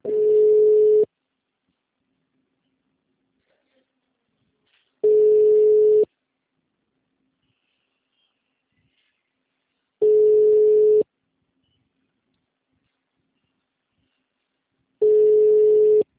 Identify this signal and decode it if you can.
Ringback tone (Europe, 425 Hz).